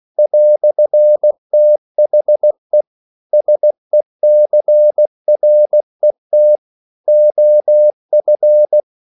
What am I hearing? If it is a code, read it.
Morse code, heard as a keyed tone, about 16 wpm, 600 Hz. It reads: "THE SECRET OF
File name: CW / Practice / EnglishQuotations0216